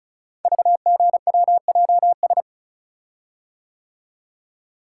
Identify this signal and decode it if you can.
Morse code, heard as a keyed tone, about 36 wpm, 690 Hz. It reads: VGWJS
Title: Code Groups - 5 Letters - 35wpm